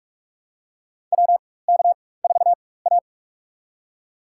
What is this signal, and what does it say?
Morse code, heard as a keyed tone, about 46 wpm, spaced as for 21 wpm, 700 Hz. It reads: WK4A
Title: USA Call Signs - 4 Characters - 4x Character Spacing - 45wpm